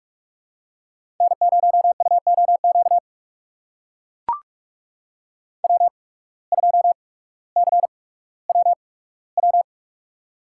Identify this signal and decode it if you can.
Morse code, heard as a keyed tone, about 44 wpm, spaced as for 19 wpm, 700 Hz. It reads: N0UOQ W2CWW
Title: USA Call Signs - 5 Characters - 8x Character Spacing - 45wpm